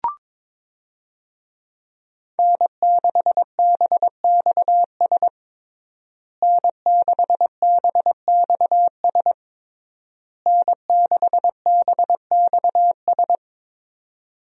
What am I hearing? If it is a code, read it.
Morse code, heard as a keyed tone, about 22 wpm, 700 Hz. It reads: N6BXS N6BXS N6BXS
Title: USA Call Signs - 5 Characters - Repeated 3x - 22wpm